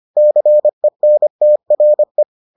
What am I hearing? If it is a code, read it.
Morse code, heard as a keyed tone, about 25 wpm, 600 Hz. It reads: CENTRE